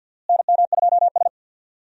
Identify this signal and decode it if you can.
Morse code, heard as a keyed tone, about 51 wpm, 700 Hz. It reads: NM2S
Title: USA Call Signs - 4 Characters - Repeated 3x - 50wpm